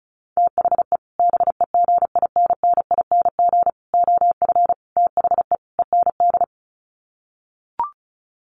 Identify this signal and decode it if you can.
Morse code, heard as a keyed tone, about 34 wpm, 700 Hz. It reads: THE BEGINNING OF THE END
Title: Sets of 5 Words Encore2 35wpm